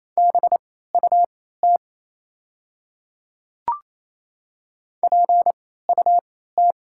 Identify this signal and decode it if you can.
Morse code, heard as a keyed tone, about 28 wpm, spaced as for 16 wpm, 690 Hz. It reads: BUT PUT
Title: 3x Character Spacing - Top 500 Words - 28wpm